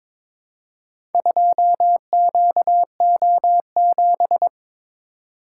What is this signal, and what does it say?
Morse code, heard as a keyed tone, about 22 wpm, 700 Hz. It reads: 2QO7
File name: practice / 22wpm / Four Letter-Number Combinations - 22wpm